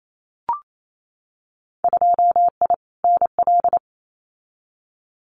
Morse code, heard as a keyed tone, about 28 wpm, 700 Hz. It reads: 2I NL
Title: Winter Field Day Exchanges Encore 28wpm